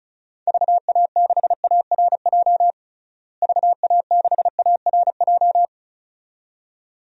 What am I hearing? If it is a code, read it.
Morse code, heard as a keyed tone, about 36 wpm, 700 Hz. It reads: VA6ARJ VA6ARJ
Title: DXCC-100 Call Signs - 6 Characters - Repeated 2x - 35wpm